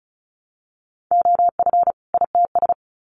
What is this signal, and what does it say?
Morse code, heard as a keyed tone, about 34 wpm, 700 Hz. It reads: OF ITS
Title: Sets of 2 Words 35wpm